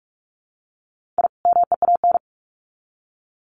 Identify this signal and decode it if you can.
Morse code, heard as a keyed tone, about 44 wpm, 700 Hz. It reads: I MEAN
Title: Sets of 2 Words 45wpm